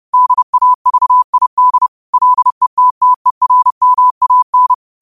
Morse code, heard as a keyed tone, about 30 wpm, 1000 Hz. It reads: DAVID LETTERMAN
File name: qotdfiles / 30wpm